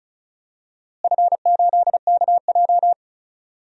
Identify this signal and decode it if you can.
Morse code, heard as a keyed tone, about 34 wpm, 700 Hz. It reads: F8KJ
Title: DXCC-100 Call Signs - 4 Characters - Repeated 2x - 35wpm